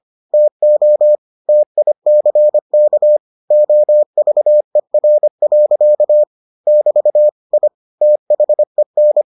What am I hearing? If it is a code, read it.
Morse code, heard as a keyed tone, about 25 wpm, 600 Hz. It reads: TO TICK OVER. <BT> I THEN